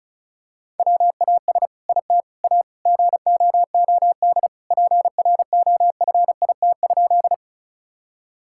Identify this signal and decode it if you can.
Morse code, heard as a keyed tone, about 36 wpm, 700 Hz. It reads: WAS IT A GOOD PROFIT?
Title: Sentences from Top 1000 Words - Review Part 5 - 35wpm